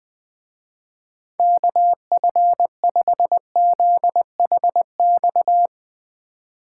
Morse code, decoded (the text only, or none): KF5ZHX